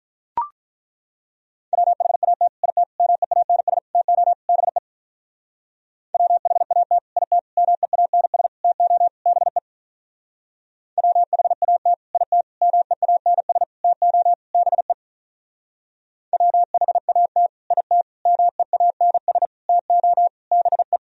Morse code, heard as a keyed tone, about 42 wpm, 700 Hz. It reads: WHAT IT MEANS TO BE WHAT IT MEANS TO BE WHAT IT MEANS TO BE WHAT IT MEANS TO BE
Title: Speed-Racing - Sets of 5 Words 35wpm